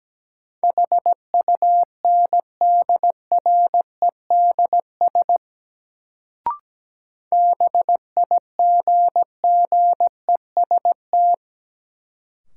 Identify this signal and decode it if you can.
Morse code, heard as a keyed tone, about 17 wpm, 700 Hz. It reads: HUNDREDS BIGGEST
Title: Sets of 4 Words - New Word Drill - 17wpm